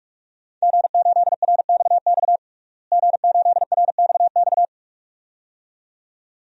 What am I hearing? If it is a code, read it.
Morse code, heard as a keyed tone, about 46 wpm, 700 Hz. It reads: G8RXX G8RXX
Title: DXCC-100 Call Signs - 5 Characters - Repeated 2x - 45wpm